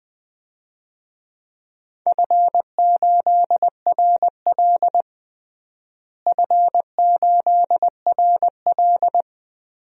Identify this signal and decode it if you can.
Morse code, heard as a keyed tone, about 20 wpm, 700 Hz. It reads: F8RL F8RL
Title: DXCC-100 Call Signs - 4 Characters - Repeated 2x - 20wpm